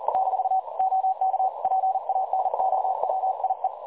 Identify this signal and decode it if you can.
Morse code, heard as a keyed tone, about 38 wpm, 760 Hz. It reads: - OK PANOWIE
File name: dlazaawansowanych / nagraniazpasma / websdr recording 3565.0kHz